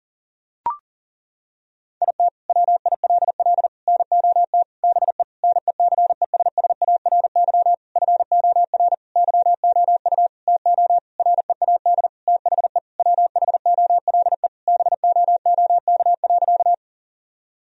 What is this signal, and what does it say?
Morse code, heard as a keyed tone, about 40 wpm, 700 Hz. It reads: IT WILL NOT BE NECESSARY FOR YOU TO READ THE WHOLE BOOK.
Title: Sentences from Top 1000 Words - Review Part 1 - 40wpm